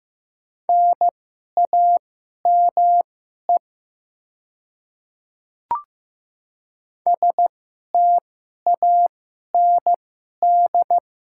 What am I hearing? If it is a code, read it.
Morse code, heard as a keyed tone, about 15 wpm, spaced as for 11 wpm, 700 Hz. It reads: NAME STAND
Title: 2x Character Spacing - Top 300 Words - 15wpm